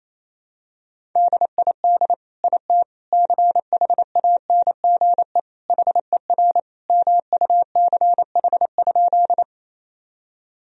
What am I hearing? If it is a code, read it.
Morse code, heard as a keyed tone, about 28 wpm, 700 Hz. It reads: DID IT CHANGE HER MUCH?
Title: Sentences from Top 1000 Words - Review Part 1 - 28wpm